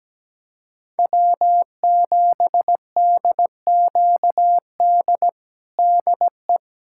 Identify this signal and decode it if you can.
Morse code, heard as a keyed tone, about 17 wpm, 700 Hz. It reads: W7DQD DE
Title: USA Callsign DE Callsign - 5 Characters - 17wpm